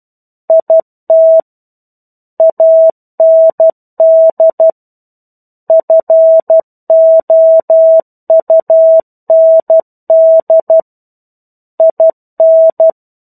Morse code, heard as a keyed tone, about 12 wpm, 650 Hz. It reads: IT AND FOUND IN
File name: 01-Chapter-01 12 WPM